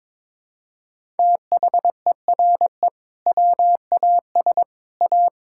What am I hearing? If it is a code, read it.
Morse code, heard as a keyed tone, about 22 wpm, 700 Hz. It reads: THERE WAS A
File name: practice / 22wpm / Sentences with 2-Word Phrases - 22wpm